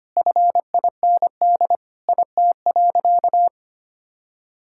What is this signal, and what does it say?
Morse code, heard as a keyed tone, about 25 wpm, 700 Hz. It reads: FIND IT.
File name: practice / 25wpm / Sentences with 4-Word Phrases - 25wpm